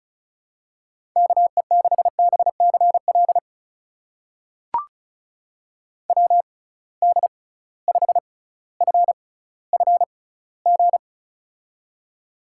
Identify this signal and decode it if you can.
Morse code, heard as a keyed tone, about 34 wpm, spaced as for 18 wpm, 700 Hz. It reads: KE6BCL WD5FFG
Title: USA Call Signs - 6 Characters - 6x Character Spacing - 35wpm